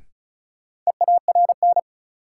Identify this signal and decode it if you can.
Morse code, heard as a keyed tone, about 36 wpm, 700 Hz. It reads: EARN
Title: Sentences from Top 1500 Words - New Word Drill Alt - 35wpm